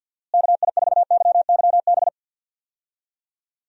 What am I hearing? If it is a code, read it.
Morse code, heard as a keyed tone, about 50 wpm, 700 Hz. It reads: KI4YYB
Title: USA Call Signs - 6 Characters - Repeated 3x - 50wpm